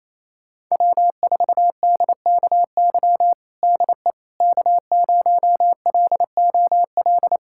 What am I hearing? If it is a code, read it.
Morse code, heard as a keyed tone, about 28 wpm, 700 Hz. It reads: W4DKY DE K0LOL
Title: USA Callsign DE Callsign - 5 Characters - 28wpm